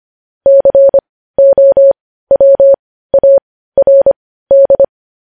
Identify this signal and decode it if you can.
Morse code, heard as a keyed tone, about 25 wpm, spaced as for 15 wpm, 550 Hz. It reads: COWARD
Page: news-headlines-in-morse-code-at-15-wpm